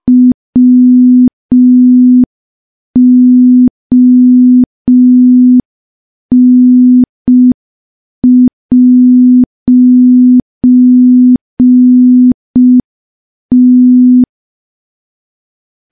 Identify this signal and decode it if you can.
Morse code, heard as a keyed tone, about 5 wpm, 250 Hz. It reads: WON'T
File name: morse / morse-training-05WPM